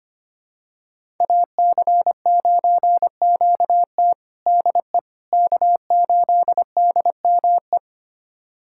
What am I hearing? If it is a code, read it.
Morse code, heard as a keyed tone, about 25 wpm, 700 Hz. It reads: AC9QT DE K8DME
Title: USA Callsign DE Callsign - 5 Characters - 25wpm